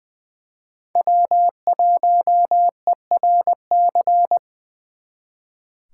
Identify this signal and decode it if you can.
Morse code, heard as a keyed tone, about 25 wpm, 700 Hz. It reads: W1ERC
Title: Speed Racing - USA 1x3 Call Signs 20wpm